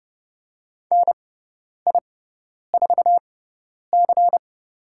Morse code, heard as a keyed tone, about 30 wpm, spaced as for 10 wpm, 700 Hz. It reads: NI4C
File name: Lesson 17g - Letter u - Call Signs 30wpm